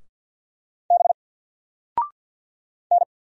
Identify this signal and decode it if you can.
Morse code, heard as a keyed tone, about 50 wpm, 700 Hz. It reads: B N
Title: Single Character (Letter:Number:Punctuation:Prosign) 50wpm